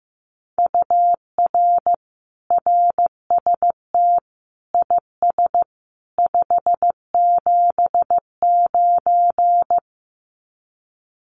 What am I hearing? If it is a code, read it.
Morse code, heard as a keyed tone, about 15 wpm, 700 Hz. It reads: UR RST IS 579
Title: Common QSO Phrases 15wpm